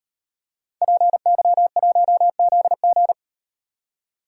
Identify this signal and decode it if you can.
Morse code, heard as a keyed tone, about 38 wpm, 700 Hz. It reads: PY1ZG